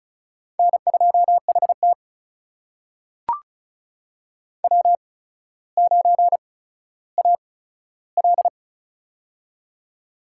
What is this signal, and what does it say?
Morse code, heard as a keyed tone, about 36 wpm, spaced as for 15 wpm, 700 Hz. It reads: N2HT W9AL
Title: USA Call Signs - 4 Characters - 8x Character Spacing - 35wpm